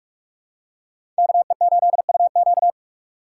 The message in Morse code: KE8UQ

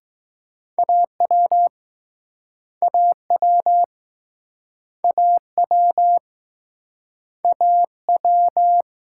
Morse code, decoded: AW AW AW AW

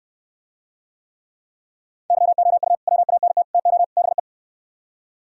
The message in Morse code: YOU WANT TO BE

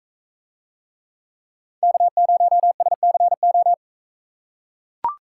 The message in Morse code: K0SCO

